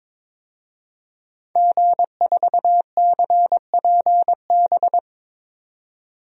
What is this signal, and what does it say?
Morse code, heard as a keyed tone, about 22 wpm, 700 Hz. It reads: G4CPB
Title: DXCC-100 Call Signs - 5 Characters - 22wpm